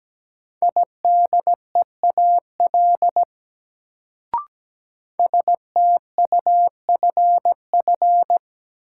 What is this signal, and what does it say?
Morse code, heard as a keyed tone, about 17 wpm, 700 Hz. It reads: IDEAL STUFF
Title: Sentences from Top 1400 Words - New Word Drill - 17wpm